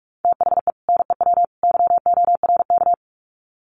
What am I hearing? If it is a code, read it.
Morse code, heard as a keyed tone, about 44 wpm, 700 Hz. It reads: THE NEW YORK
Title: Sets of 3 Words 45wpm